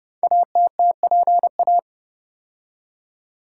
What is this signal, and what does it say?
Morse code, heard as a keyed tone, about 30 wpm, 700 Hz. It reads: ATTPA
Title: Code Groups - 5 Letters - 30wpm